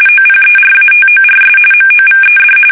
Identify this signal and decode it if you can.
RTTY, about 45.45 baud. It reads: OPERATORS MOVED
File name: fsk850 / teletype-RTTY